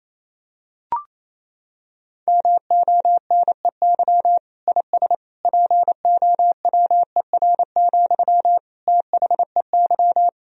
Morse code, heard as a keyed tone, about 28 wpm, 700 Hz. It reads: MONEY IS POWER, THEY
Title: Sentences from Top 1000 Words - Review Part 2 - 28wpm